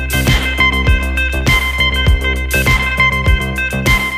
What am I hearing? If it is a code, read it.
Emergency Alert System (SAME) tones, heard as harsh buzzing bursts. It is heard from 0.4 to 3.2 s.